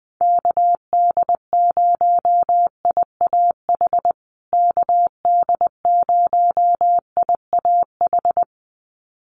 Morse code, heard as a keyed tone, about 20 wpm, 700 Hz. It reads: KD0IAH KD0IAH
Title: USA Call Signs Repeated 2x 20wpm